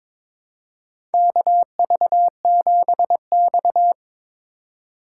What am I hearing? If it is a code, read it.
Morse code, heard as a keyed tone, about 22 wpm, 700 Hz. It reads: KV7X
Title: USA Call Signs - 4 Characters - 2x Character Spacing - 22wpm